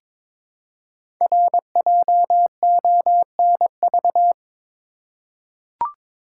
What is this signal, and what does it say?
Morse code, heard as a keyed tone, about 21 wpm, 700 Hz. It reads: RJONV
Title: Code Groups - 5 Letters - 22wpm